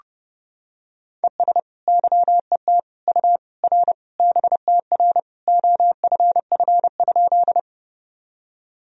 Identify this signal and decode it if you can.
Morse code, heard as a keyed tone, about 30 wpm, 700 Hz. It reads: ES YET U R BTR OFF?
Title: Sentences from Top 500 Words and Abbreviations 30wpm